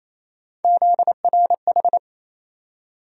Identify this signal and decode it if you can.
Morse code, heard as a keyed tone, about 28 wpm, 700 Hz. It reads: ZRH
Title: USA Call Sign Postfixes - 3 Letters 28wpm